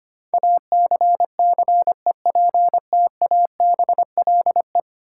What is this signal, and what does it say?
Morse code, heard as a keyed tone, about 25 wpm, 690 Hz. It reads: ACCEPTABLE
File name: 10 Letter Words - First 3 Letters Given - 25wpm